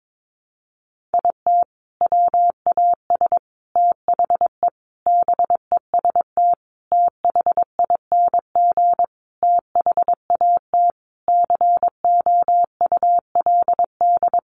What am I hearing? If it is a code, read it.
Morse code, heard as a keyed tone, about 22 wpm, 700 Hz. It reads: IT WAS THE BEST THING THAT COULD
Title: Sentences from Top 400 Words 22wpm